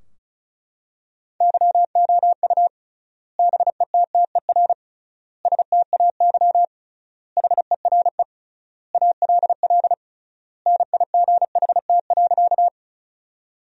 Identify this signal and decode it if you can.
Morse code, heard as a keyed tone, about 36 wpm, 700 Hz. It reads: YOU BETTER STAY HERE ALL NIGHT.
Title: 3x Word-Spacing - Sentences from Top 500 Words - 35wpm